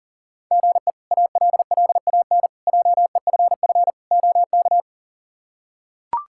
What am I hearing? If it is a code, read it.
Morse code, heard as a keyed tone, about 40 wpm, 690 Hz. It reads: GE ALLAN JEFF OK
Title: K1USN SST QSO Practice - 40wpm